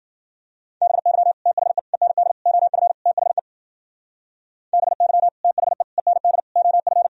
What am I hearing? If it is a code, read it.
Morse code, heard as a keyed tone, about 53 wpm, 700 Hz. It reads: BY THE END OF THE BY THE END OF